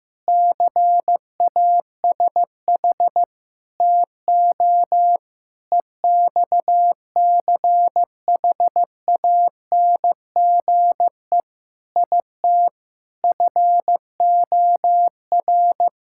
Morse code, heard as a keyed tone, about 15 wpm, 700 Hz. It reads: CASH TO EXCHANGE IT FOR
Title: Sentences from Top 1000 Words - Review Part 6 - 15wpm